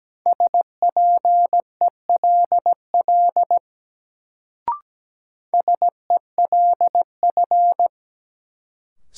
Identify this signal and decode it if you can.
Morse code, heard as a keyed tone, about 17 wpm, 700 Hz. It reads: SPELL SELF
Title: Sentences from Top 1000 Words - Review Part 1 - New Word Drill - 17wpm